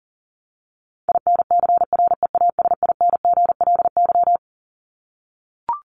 Morse code, heard as a keyed tone, about 40 wpm, 700 Hz. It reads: INCREASINGLY
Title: LY Suffix - Top 100 Words with Base Words 40wpm